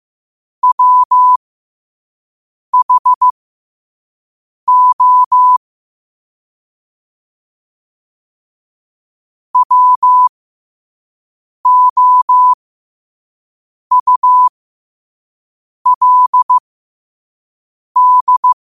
Morse code, heard as a keyed tone, about 15 wpm, spaced as for 5 wpm, 1000 Hz. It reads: WHO WOULD
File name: qotdfiles / 5wpm